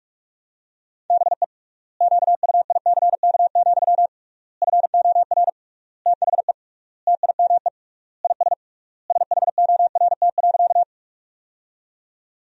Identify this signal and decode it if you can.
Morse code, heard as a keyed tone, about 46 wpm, 700 Hz. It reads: BE QUICK, FOR THE TIME IS SHORT.
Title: 3x Word-Spacing - Sentences from Top 500 Words - 45wpm